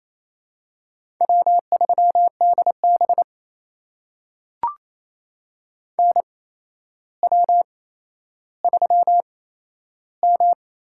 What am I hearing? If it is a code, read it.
Morse code, heard as a keyed tone, about 27 wpm, spaced as for 12 wpm, 700 Hz. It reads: W3DB NW3M
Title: USA Call Signs - 4 Characters - 8x Character Spacing - 28wpm